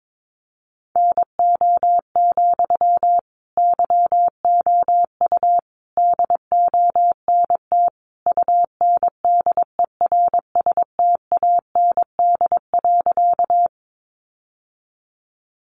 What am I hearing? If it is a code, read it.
Morse code, heard as a keyed tone, about 22 wpm, 700 Hz. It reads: NO, YOU DONT UNDERSTAND.
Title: Sentences from Top 600 Words 22wpm